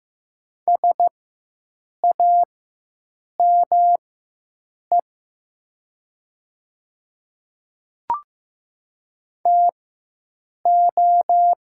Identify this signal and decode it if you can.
Morse code, heard as a keyed tone, about 15 wpm, spaced as for 7 wpm, 700 Hz. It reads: SAME TO
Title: 4x Character Spacing - Top 200 Words - 15wpm